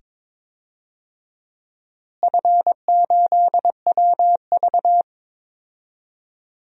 Morse code, heard as a keyed tone, about 22 wpm, 700 Hz. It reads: F8WV
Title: DXCC-100 Call Signs - 4 Characters - Repeated 2x - 22wpm